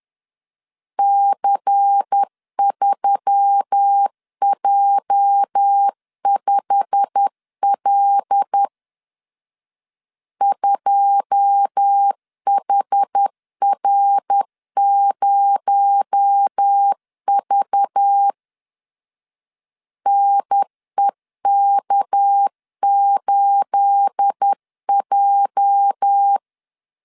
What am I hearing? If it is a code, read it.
Morse code, heard as a keyed tone, about 10 wpm, 800 Hz. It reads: C3J5L 2HR0V NEK8J
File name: CW / DET / 1 9 1500C S B 10wpm 4439